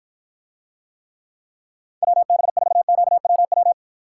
Morse code, heard as a keyed tone, about 53 wpm, 700 Hz. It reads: WB4QWW